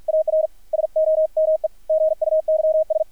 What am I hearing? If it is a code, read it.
Morse code, heard as a keyed tone, about 50 wpm, 630 Hz, with radio noise. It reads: AR SOME GUYS
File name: archived / CRSnet